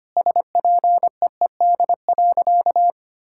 Morse code, heard as a keyed tone, about 25 wpm, 700 Hz. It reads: SPEED.
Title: Sentences from Top 1000 Words - Review Part 5 - 25wpm